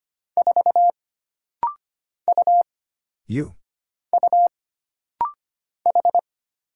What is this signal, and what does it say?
Morse code, heard as a keyed tone, about 25 wpm, 700 Hz. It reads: 4 U U H
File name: Single Character (Letter:Number:Punctuation:Prosign) - Rapid-Fire - 25wpm